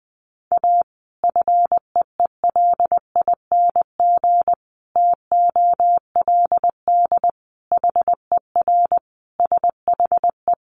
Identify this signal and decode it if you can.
Morse code, heard as a keyed tone, about 19 wpm, 700 Hz. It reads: A FEELING TOLD HER SHE
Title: Sentences from Top 1200 Words 20wpm